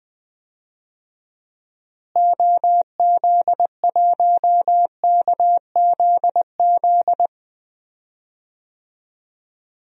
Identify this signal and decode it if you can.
Morse code, heard as a keyed tone, about 20 wpm, 700 Hz. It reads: OZ1KZZ
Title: DXCC-100 Call Signs - 6 Characters - Repeated 2x - 20wpm